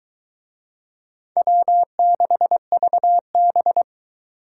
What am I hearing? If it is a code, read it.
Morse code, heard as a keyed tone, about 23 wpm, 700 Hz. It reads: W6VB